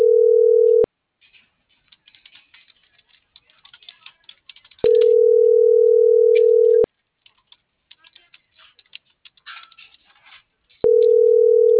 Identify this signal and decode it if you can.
Ringback tone (North America), 440 + 480 Hz.